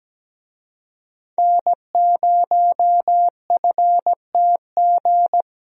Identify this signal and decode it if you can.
Morse code, heard as a keyed tone, about 17 wpm, 700 Hz. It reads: N0FTG
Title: USA Call Signs - 5 Characters - Repeated 2x - 17wpm